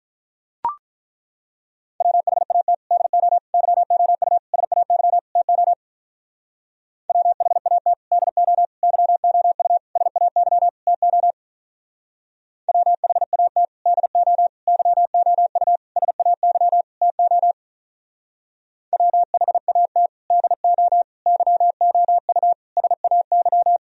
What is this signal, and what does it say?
Morse code, heard as a keyed tone, about 43 wpm, 700 Hz. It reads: WHAT DO YOU SAY TO WHAT DO YOU SAY TO WHAT DO YOU SAY TO WHAT DO YOU SAY
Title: Speed-Racing - Sets of 5 Words 35wpm